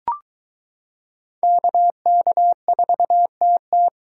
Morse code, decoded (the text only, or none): KK4TT